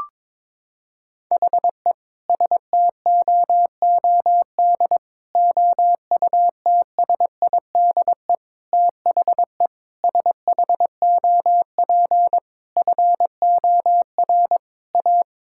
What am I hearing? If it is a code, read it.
Morse code, heard as a keyed tone, about 22 wpm, 700 Hz. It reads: HE STOOD OUTSIDE THE SHOP FOR A
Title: Sentences from Top 1600 Words 22wpm